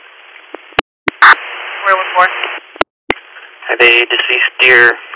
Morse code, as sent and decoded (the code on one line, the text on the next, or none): none